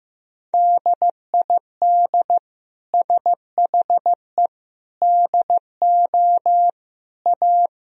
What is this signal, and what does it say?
Morse code, heard as a keyed tone, about 15 wpm, 700 Hz. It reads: DID SHE DO A
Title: Sentences from Top 1000 Words - Review Part 3 - 15wpm